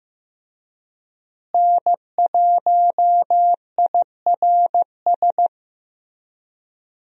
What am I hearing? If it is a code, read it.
Morse code, heard as a keyed tone, about 15 wpm, 700 Hz. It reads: N1IRS
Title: USA Call Signs - 5 Characters - Repeated 3x - 15wpm